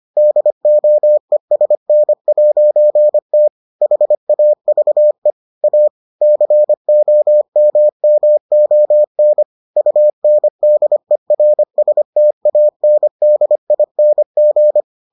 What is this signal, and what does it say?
Morse code, heard as a keyed tone, about 25 wpm, 600 Hz. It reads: DOESN'T HAVE A COMMON UNDERSTANDING